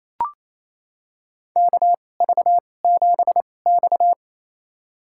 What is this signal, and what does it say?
Morse code, heard as a keyed tone, about 28 wpm, spaced as for 20 wpm, 700 Hz. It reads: KV7X
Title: USA Call Signs - 4 Characters - 2x Character Spacing - 28wpm